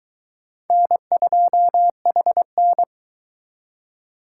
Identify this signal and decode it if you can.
Morse code, heard as a keyed tone, about 22 wpm, 700 Hz. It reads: N2HN